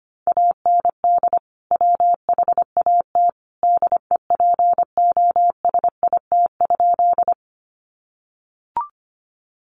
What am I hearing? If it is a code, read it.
Morse code, heard as a keyed tone, about 25 wpm, 700 Hz. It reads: AND WHAT DEPOSIT?
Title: Sentences from Top 1300 Words 25wpm